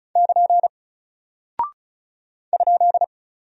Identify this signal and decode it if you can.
Morse code, heard as a keyed tone, about 36 wpm, 700 Hz. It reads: <KN> ?
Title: Single Character (Letter:Number:Punctuation:Prosign) Alt 35wpm